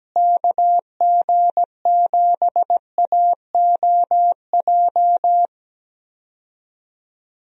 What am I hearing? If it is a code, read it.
Morse code, heard as a keyed tone, about 17 wpm, 700 Hz. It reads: KG7AOJ
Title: USA Call Signs - 6 Characters - Repeated 3x - 17wpm